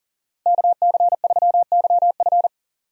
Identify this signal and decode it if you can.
Morse code, heard as a keyed tone, about 40 wpm, 700 Hz. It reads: KC3YF
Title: USA Call Signs - 5 Characters - 40wpm